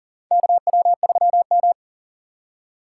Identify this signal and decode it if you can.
Morse code, heard as a keyed tone, about 40 wpm, 700 Hz. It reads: KW3M
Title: USA Call Signs - 4 Characters - 6x Character Spacing - 40wpm